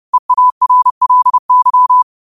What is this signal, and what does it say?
Morse code, heard as a keyed tone, about 30 wpm, 1000 Hz. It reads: EARLY
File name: qotdfiles / 30wpm